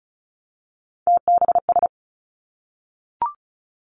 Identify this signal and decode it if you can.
Morse code, heard as a keyed tone, about 36 wpm, 700 Hz. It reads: TBS
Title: Three Challenging Letter-Number Combinations 35wpm